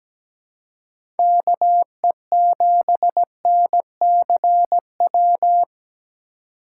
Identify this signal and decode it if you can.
Morse code, heard as a keyed tone, about 17 wpm, 700 Hz. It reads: KE7NCW
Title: USA Call Signs - 6 Characters - 3x Character Spacing - 17wpm